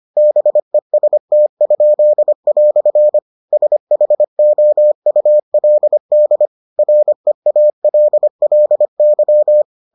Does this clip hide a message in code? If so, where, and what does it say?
Yes: Morse code from 0.2 to 9.6 s, heard as a keyed tone, about 25 wpm, 600 Hz. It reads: BEST?" SHOULD REALLY